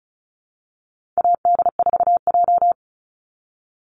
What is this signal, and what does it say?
Morse code, heard as a keyed tone, about 34 wpm, 700 Hz. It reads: AD4J
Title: Call Signs - 35wpm